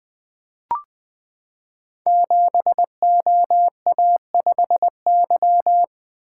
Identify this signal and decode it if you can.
Morse code, heard as a keyed tone, about 20 wpm, 700 Hz. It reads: 7OA5Y
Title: Code Groups - 5 Letter:Numbers - 20wpm